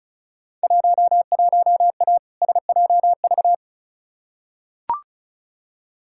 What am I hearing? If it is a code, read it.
Morse code, heard as a keyed tone, about 36 wpm, 700 Hz. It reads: 11A SJV
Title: Field Day Exchanges Encore - 35wpm